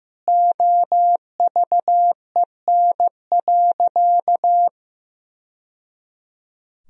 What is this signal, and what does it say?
Morse code, heard as a keyed tone, about 15 wpm, 700 Hz. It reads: OVEN.
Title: Sentences from Top 1000 Words - Review Part 4 - 15wpm